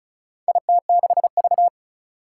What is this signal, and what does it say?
Morse code, heard as a keyed tone, about 36 wpm, 700 Hz. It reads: IT6V
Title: Four Challenging Letter-Number Combinations 35wpm